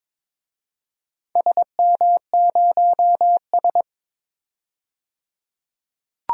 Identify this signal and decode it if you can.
Morse code, heard as a keyed tone, about 22 wpm, 700 Hz. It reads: SM0S
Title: DXCC-100 Call Signs - 4 Characters - 22wpm